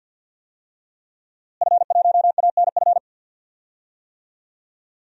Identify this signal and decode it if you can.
Morse code, heard as a keyed tone, about 49 wpm, 700 Hz. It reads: F1ANF
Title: DXCC-100 Call Signs - 5 Characters - Repeated 3x - 50wpm